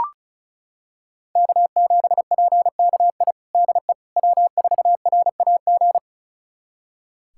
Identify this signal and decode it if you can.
Morse code, heard as a keyed tone, about 34 wpm, 700 Hz. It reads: K7PKI DE W4RAG
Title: USA Callsign DE Callsign - 5 Characters - 35wpm